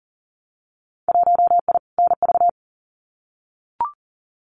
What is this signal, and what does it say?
Morse code, heard as a keyed tone, about 40 wpm, 700 Hz. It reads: 1I NV
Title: Winter Field Day Exchanges Encore 40wpm